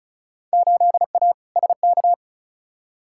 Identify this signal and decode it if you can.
Morse code, heard as a keyed tone, about 36 wpm, 700 Hz. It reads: 8A SK
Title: Field Day Exchanges Encore - 35wpm